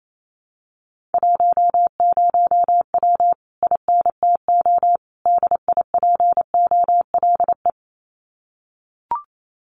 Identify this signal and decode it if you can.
Morse code, heard as a keyed tone, about 28 wpm, 700 Hz. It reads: 10W INTO DIPOLE
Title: Common QSO Phrases Encore 28wpm